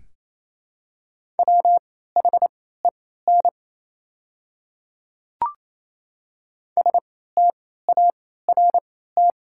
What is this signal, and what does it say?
Morse code, heard as a keyed tone, about 28 wpm, spaced as for 16 wpm, 690 Hz. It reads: WHEN START
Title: 3x Character Spacing - Top 300 Words - 28wpm